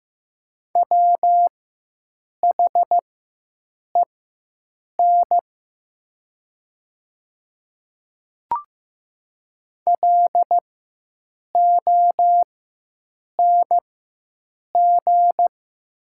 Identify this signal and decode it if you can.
Morse code, heard as a keyed tone, about 15 wpm, spaced as for 7 wpm, 700 Hz. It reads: WHEN LONG
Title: 4x Character Spacing - Top 100 Words - 15wpm